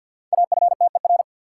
Morse code, heard as a keyed tone, about 53 wpm, 700 Hz. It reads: AFTER
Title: Sets of 2 Words - New Word Drill - 50wpm